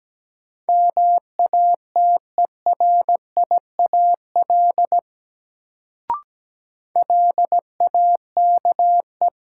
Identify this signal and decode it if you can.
Morse code, heard as a keyed tone, about 17 wpm, 700 Hz. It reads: MATERIAL LAKE
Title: Sentences from Top 1100 Words - New Word Drill Alt - 17wpm